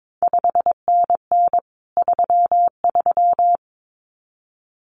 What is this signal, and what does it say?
Morse code, heard as a keyed tone, about 22 wpm, 700 Hz. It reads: 5NN 33
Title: CQ WW DX Contest Exchange Encore - 22wpm